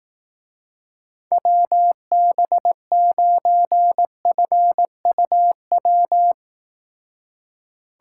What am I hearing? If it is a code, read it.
Morse code, heard as a keyed tone, about 18 wpm, 700 Hz. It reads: WB9FUW
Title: Speed Racing - USA 2x3 Call Signs 15wpm